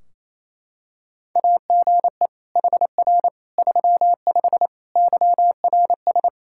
Morse code, heard as a keyed tone, about 28 wpm, 700 Hz. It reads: AGE HR 35 YRS
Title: Practice QSO - Exchange 3 - More Info - 28wpm